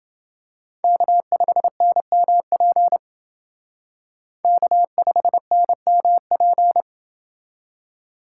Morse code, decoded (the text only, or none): K5NMP K5NMP